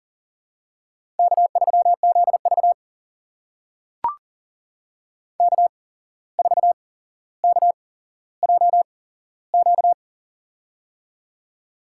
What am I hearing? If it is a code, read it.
Morse code, heard as a keyed tone, about 40 wpm, spaced as for 16 wpm, 700 Hz. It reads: K3ZV K4KJQ
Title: USA Call Signs - 8x Character Spacing - 40wpm